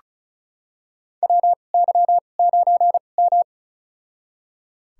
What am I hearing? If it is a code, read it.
Morse code, heard as a keyed tone, about 34 wpm, spaced as for 25 wpm, 700 Hz. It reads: WY9M
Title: USA Call Signs - 4 Characters - 2x Character Spacing - 35wpm